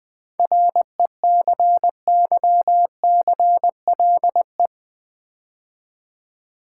Morse code, heard as a keyed tone, about 20 wpm, 690 Hz. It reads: RECYCLE
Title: 7 Letter Words - First 3 Letters Given - 20wpm